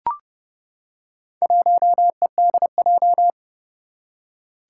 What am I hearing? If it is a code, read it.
Morse code, heard as a keyed tone, about 30 wpm, 700 Hz. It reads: 1EDJ
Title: Four Challenging Letter-Number Combinations 30wpm